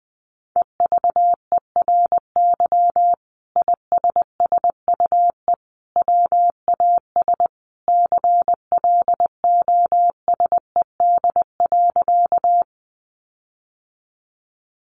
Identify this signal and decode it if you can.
Morse code, heard as a keyed tone, about 20 wpm, 700 Hz. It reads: EVERY ISSUE WAS CLOSED.
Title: Sentences from Top 1400 Words 20wpm